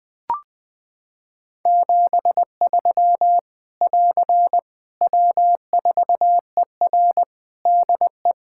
Morse code, heard as a keyed tone, about 20 wpm, 700 Hz. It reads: 73 <AR> W4ER DE
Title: K1USN SST QSO Practice 20wpm